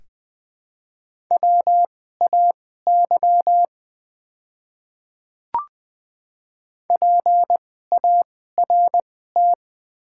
Morse code, heard as a keyed tone, about 20 wpm, spaced as for 14 wpm, 700 Hz. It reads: WAY PART
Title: 2x Character Spacing - Top 200 Words - 20wpm